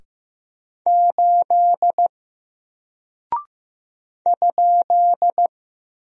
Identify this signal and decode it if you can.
Morse code, heard as a keyed tone, about 15 wpm, 700 Hz. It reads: 8 ?
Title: Single Character (Letter:Number:Punctuation:Prosign) 15wpm